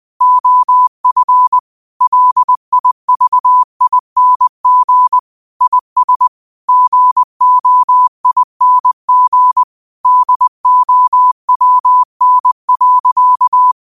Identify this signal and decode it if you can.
Morse code, heard as a keyed tone, about 20 wpm, 1000 Hz. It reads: OF LIVING IS GOING DOWN.